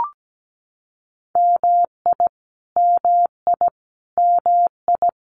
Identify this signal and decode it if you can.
Morse code, heard as a keyed tone, about 17 wpm, 700 Hz. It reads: MI MI MI
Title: US States Repeated 3x 17wpm